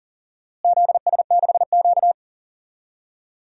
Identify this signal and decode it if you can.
Morse code, heard as a keyed tone, about 40 wpm, 700 Hz. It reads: ZS6Q
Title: DXCC-100 Call Signs - 4 Characters - Repeated 2x - 40wpm